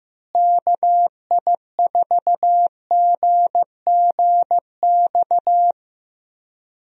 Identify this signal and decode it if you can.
Morse code, heard as a keyed tone, about 15 wpm, 700 Hz. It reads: KI4GGX
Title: USA Call Signs - 6 Characters - Repeated 3x - 15wpm